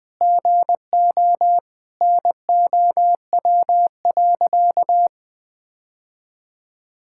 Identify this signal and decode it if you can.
Morse code, heard as a keyed tone, about 21 wpm, 700 Hz. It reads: GO NOW.
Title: Sentences with 3-Word Phrases - Spoken Beginning - 20wpm